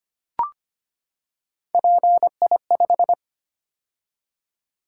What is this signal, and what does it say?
Morse code, heard as a keyed tone, about 25 wpm, 700 Hz. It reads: PI5
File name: Three Letter-Number Combinations - 25wpm